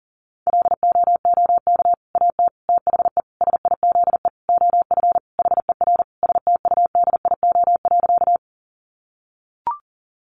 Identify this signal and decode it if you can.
Morse code, heard as a keyed tone, about 40 wpm, 700 Hz. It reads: LOOK AT THE SIZE OF HER STUDIO.
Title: Sentences from Top 1200 Words 40wpm